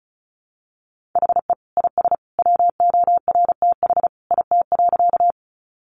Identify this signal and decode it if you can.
Morse code, heard as a keyed tone, about 34 wpm, 700 Hz. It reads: HE IS WORTH IT.
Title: Sentences from Top 800 Words 35wpm